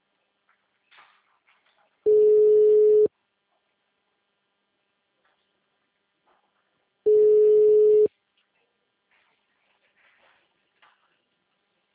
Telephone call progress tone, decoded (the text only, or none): ringback tone